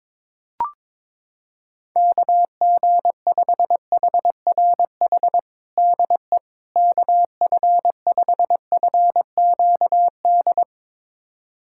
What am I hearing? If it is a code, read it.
Morse code, heard as a keyed tone, about 22 wpm, 700 Hz. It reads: KG5HRH DE KF5FQD
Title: USA Callsign DE Callsign - 6 Characters - 22wpm